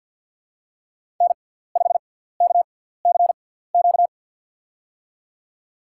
Morse code, heard as a keyed tone, about 53 wpm, spaced as for 17 wpm, 700 Hz. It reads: N5KCQ
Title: USA Call Signs - 6x Character Spacing - 50wpm